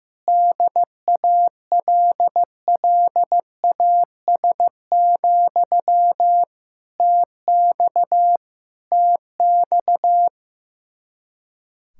Morse code, heard as a keyed tone, about 15 wpm, 700 Hz. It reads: DALLAS, TX TX
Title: Practice QSO - Exchange 1 - Beginning - 15wpm